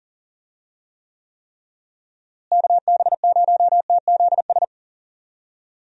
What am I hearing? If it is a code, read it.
Morse code, heard as a keyed tone, about 40 wpm, 700 Hz. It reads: KB0TZS
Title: USA Call Signs - 6 Characters - Repeated 3x - 40wpm